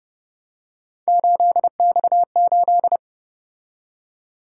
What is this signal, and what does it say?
Morse code, heard as a keyed tone, about 29 wpm, 700 Hz. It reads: 8X8
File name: Speed-Racing - Three Letter-Number Combinations - 25wpm